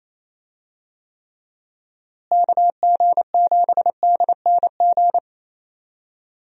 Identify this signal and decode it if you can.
Morse code, heard as a keyed tone, about 28 wpm, 700 Hz. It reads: KG7DNG
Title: USA Call Signs - 6 Characters - Repeated 2x - 28wpm